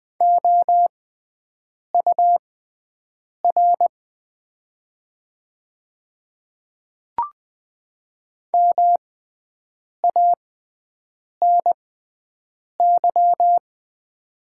Morse code, heard as a keyed tone, about 20 wpm, spaced as for 7 wpm, 700 Hz. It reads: OUR MANY
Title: 6x Character Spacing - Top 200 Words - 20wpm